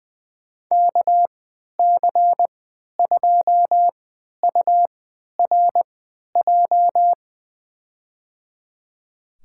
Morse code, heard as a keyed tone, about 20 wpm, spaced as for 11 wpm, 700 Hz. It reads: KC2URJ
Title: USA Call Signs - 3x Character Spacing - 20wpm